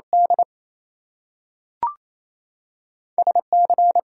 Morse code, heard as a keyed tone, about 28 wpm, 700 Hz. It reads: D SC
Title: US States and Canadian Provinces 28wpm